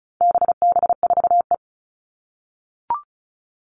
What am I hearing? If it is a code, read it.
Morse code, heard as a keyed tone, about 34 wpm, 700 Hz. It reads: BB4E
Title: All Cut Numbers - 4-Digit 35wpm